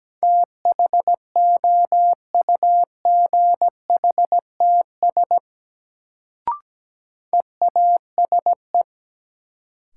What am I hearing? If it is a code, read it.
Morse code, heard as a keyed tone, about 17 wpm, 700 Hz. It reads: THOUGHTS EASE
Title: Sentences from Top 1000 Words - Review Part 1 - New Word Drill - 17wpm